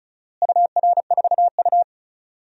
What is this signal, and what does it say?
Morse code, heard as a keyed tone, about 36 wpm, 700 Hz. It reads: UR4U
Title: DXCC-100 Call Signs - 4 Characters - Repeated 2x - 35wpm